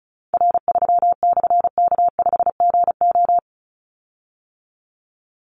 Morse code, heard as a keyed tone, about 36 wpm, 700 Hz. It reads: R3/K5GO
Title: Call Signs - 35wpm